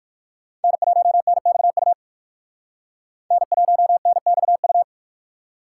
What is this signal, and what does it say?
Morse code, heard as a keyed tone, about 60 wpm, 700 Hz. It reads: N1NXU N1NXU